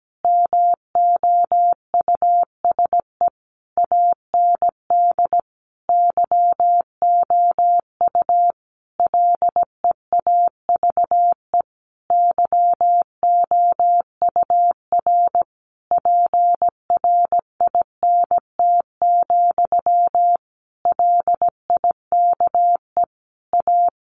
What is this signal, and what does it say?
Morse code, heard as a keyed tone, about 17 wpm, 700 Hz. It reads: MOUSE AND YOU LEAVE YOUR PRINT, LIKE A
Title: A Sound Of Thunder 17wpm